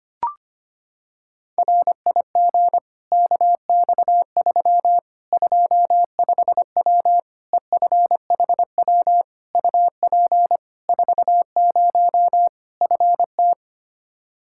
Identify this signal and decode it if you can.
Morse code, heard as a keyed tone, about 25 wpm, 700 Hz. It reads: RIG KX3 25W EFHW UP 40 FT
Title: Practice QSO - Exchange 2 - Info - 25wpm